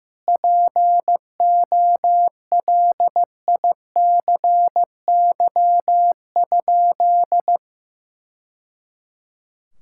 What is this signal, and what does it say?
Morse code, heard as a keyed tone, about 16 wpm, 700 Hz. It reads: POLICY?
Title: Sentences from Top 1000 Words - Review Part 5 - 15wpm